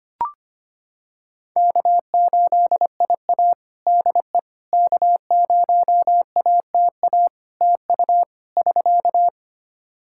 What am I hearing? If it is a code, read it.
Morse code, heard as a keyed tone, about 25 wpm, 700 Hz. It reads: K8IA DE K0ATA TU <SK>
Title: Practice QSO - Exchange 4 - Ending - 25wpm